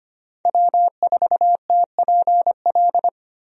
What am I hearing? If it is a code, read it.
Morse code, heard as a keyed tone, about 25 wpm, 700 Hz. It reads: W4TPL
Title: USA Call Signs - 5 Characters - 25wpm